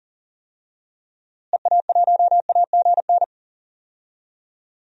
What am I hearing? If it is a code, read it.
Morse code, heard as a keyed tone, about 39 wpm, 700 Hz. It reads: EA1AGN
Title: DXCC-100 Call Signs - 6 Characters - Repeated 3x - 40wpm